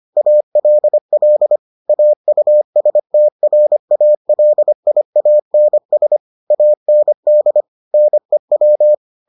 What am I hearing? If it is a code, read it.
Morse code, heard as a keyed tone, about 25 wpm, 600 Hz. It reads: ALL AUSTRALIANS AND NEW